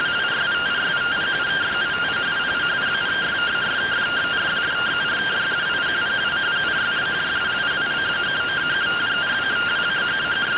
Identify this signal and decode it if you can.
Two-tone teleprinter signal, about 170 Hz shift.
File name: rtty rx